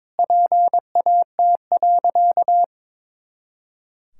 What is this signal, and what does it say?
Morse code, heard as a keyed tone, about 22 wpm, 700 Hz. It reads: PAT.
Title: Pangram Challenge v2 - 22wpm